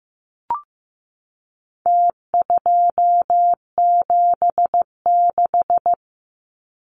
Morse code, heard as a keyed tone, about 15 wpm, 700 Hz. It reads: T276
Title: Numbers - Contest Style - 4-Digit 15wpm